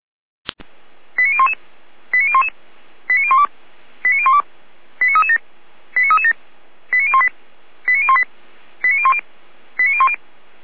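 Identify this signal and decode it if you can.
Five-tone selective call (ZVEI), heard as quick runs of five tones, tones of about 70 ms. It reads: 89011 89011 89012 89012 89337 89337 89018 89018 89010 89010
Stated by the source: Five-tone ZVEI selective call received on SDR.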